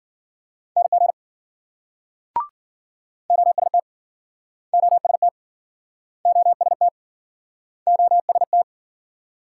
Morse code, heard as a keyed tone, about 54 wpm, 700 Hz. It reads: NG OST OST OST OST